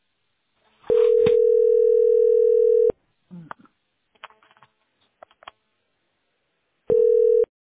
Ringback tone (North America), 440 + 480 Hz.